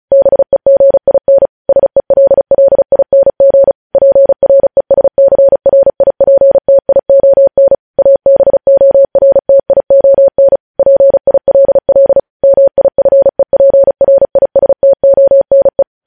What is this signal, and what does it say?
Morse code, heard as a keyed tone, about 36 wpm, 550 Hz. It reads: BEGIN SELLING PRESCRIPTION ABORTION PILL MIFEPRISTONE
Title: News Headlines at 35